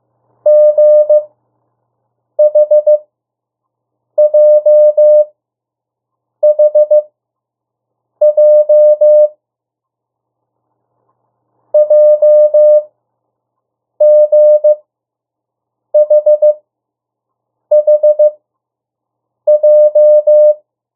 Morse code, heard as a keyed tone, about 15 wpm, spaced as for 6 wpm, 600 Hz. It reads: GHJHJ JGHHJ
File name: urok 1